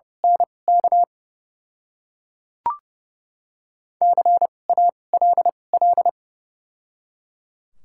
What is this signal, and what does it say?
Morse code, heard as a keyed tone, about 30 wpm, spaced as for 22 wpm, 700 Hz. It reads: NK CALL
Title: 2x Character Spacing - Top 200 Words - 30wpm